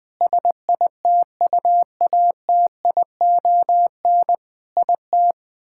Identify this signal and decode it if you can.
Morse code, heard as a keyed tone, about 20 wpm, 700 Hz. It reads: SITUATION IT
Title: Sentences with 9-Word Phrases - Spoken Beginning - 20wpm